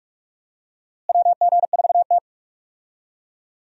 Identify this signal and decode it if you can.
Morse code, heard as a keyed tone, about 46 wpm, 700 Hz. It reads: WG4T
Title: USA Call Signs - 4 Characters - Repeated 2x - 45wpm